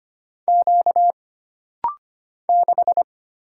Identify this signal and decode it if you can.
Morse code, heard as a keyed tone, about 25 wpm, 700 Hz. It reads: Q 6
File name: Single Character (Letter:Number:Punctuation:Prosign) - Rapid-Fire - 25wpm